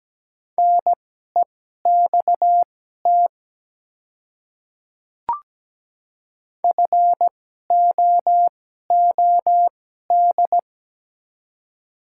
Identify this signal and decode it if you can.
Morse code, heard as a keyed tone, about 17 wpm, spaced as for 12 wpm, 690 Hz. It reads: NEXT FOOD
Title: 2x Character Spacing - Top 300 Words - 17wpm